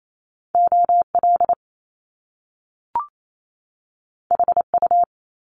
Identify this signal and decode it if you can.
Morse code, heard as a keyed tone, about 28 wpm, 700 Hz. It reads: OL HU
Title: Top Two Letter Combinations in Words 28wpm